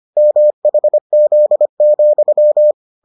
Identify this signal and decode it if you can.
Morse code, heard as a keyed tone, about 25 wpm, 600 Hz. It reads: MHZ,